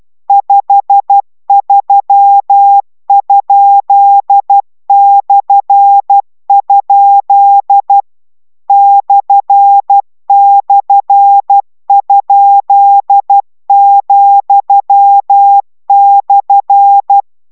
Morse code, heard as a keyed tone, about 12 wpm, 800 Hz. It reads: 53?/? //?,/